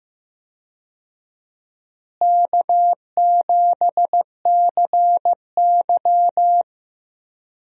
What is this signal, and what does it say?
Morse code, heard as a keyed tone, about 15 wpm, 700 Hz. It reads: K7CY
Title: USA Call Signs - 4 Characters - Repeated 3x - 15wpm